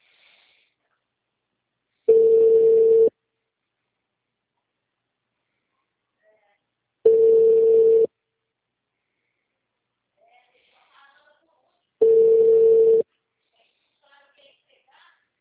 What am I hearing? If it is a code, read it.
Ringback tone (China, 450 Hz).